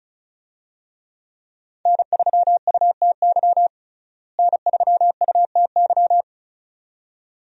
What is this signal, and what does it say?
Morse code, heard as a keyed tone, about 34 wpm, 700 Hz. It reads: N3UTY N3UTY
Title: USA Call Signs - 5 Characters - Repeated 2x - 35wpm